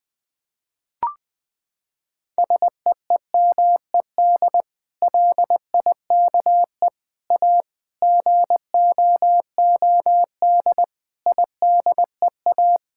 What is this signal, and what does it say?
Morse code, heard as a keyed tone, about 20 wpm, 700 Hz. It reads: SEEMED LIKE A GOOD IDEA
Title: Sets of 8 Words - 20wpm